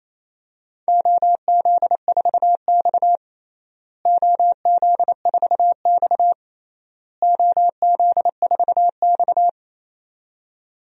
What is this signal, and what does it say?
Morse code, heard as a keyed tone, about 27 wpm, 700 Hz. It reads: OZ4X OZ4X OZ4X
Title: DXCC-100 Call Signs - 4 Characters - Repeated 3x - 28wpm